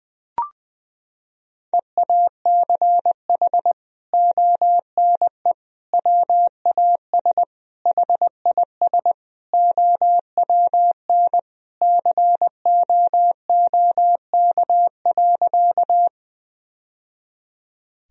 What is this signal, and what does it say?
Morse code, heard as a keyed tone, about 20 wpm, 700 Hz. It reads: EACH ONE WAS HIS OWN COOK.
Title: Sentences from Top 1000 Words - Review Part 4 - 20wpm